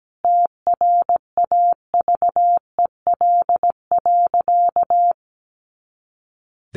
Morse code, heard as a keyed tone, about 17 wpm, 700 Hz. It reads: TRAVEL.
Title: A Sound Of Thunder 17wpm